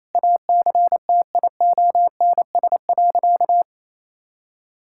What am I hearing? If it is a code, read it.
Morse code, heard as a keyed tone, about 28 wpm, 690 Hz. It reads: ACTIONS.
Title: Sentences from Top 1000 Words - Review Part 6 - 28wpm